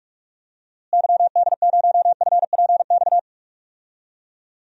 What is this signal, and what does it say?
Morse code, heard as a keyed tone, about 46 wpm, 700 Hz. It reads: YD0FPX
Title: DXCC-100 Call Signs - 6 Characters - Repeated 3x - 45wpm